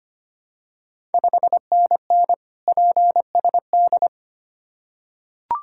Morse code, heard as a keyed tone, about 25 wpm, 700 Hz. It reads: 5NN PSD
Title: Ontario QSO Party Exchange 25wpm